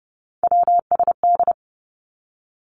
Morse code, heard as a keyed tone, about 30 wpm, 700 Hz. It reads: WSD
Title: Three Challenging Letter-Number Combinations 30wpm